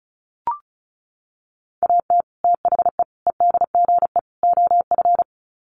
Morse code, heard as a keyed tone, about 34 wpm, 700 Hz. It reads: AT THE EDGE OF
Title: Sets of 4 Words 35wpm